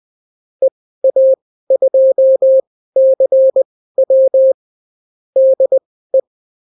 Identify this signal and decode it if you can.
Morse code, heard as a keyed tone, about 20 wpm, spaced as for 14 wpm, 530 Hz. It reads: EA2CW DE
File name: CW QSO libroCW4 20-10ppm